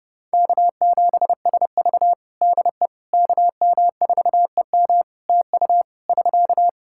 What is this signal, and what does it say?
Morse code, heard as a keyed tone, about 30 wpm, 700 Hz. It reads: K7SV DE KM4EM TU <SK>
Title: Practice QSO - Exchange 4 - Ending - 30wpm